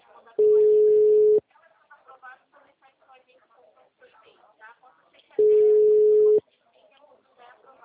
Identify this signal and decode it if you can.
Ringback tone (Europe, 425 Hz), with speech.